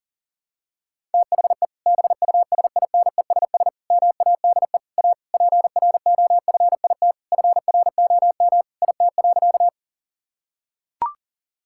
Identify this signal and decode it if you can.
Morse code, heard as a keyed tone, about 40 wpm, 700 Hz. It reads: THE BUSINESS MADE A PROFIT FROM IT.
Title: Sentences from Top 1000 Words - Review Part 5 - 40wpm